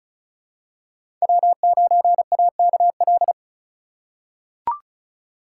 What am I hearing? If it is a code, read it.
Morse code, heard as a keyed tone, about 34 wpm, 700 Hz. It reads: W9AKL
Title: USA Call Signs - 5 Characters - 4x Character Spacing - 35wpm